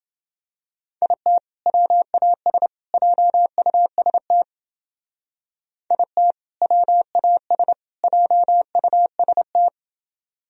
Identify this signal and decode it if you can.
Morse code, heard as a keyed tone, about 28 wpm, 700 Hz. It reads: IT WAS JUST IT WAS JUST